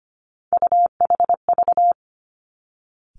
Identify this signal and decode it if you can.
Morse code, heard as a keyed tone, about 25 wpm, 700 Hz. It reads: UHV
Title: Three Challenging Letter-Number Combinations 25wpm